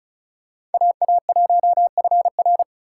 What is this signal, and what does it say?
Morse code, heard as a keyed tone, about 36 wpm, 700 Hz. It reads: AA1FR
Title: USA Call Signs - 4x Character Spacing - 35wpm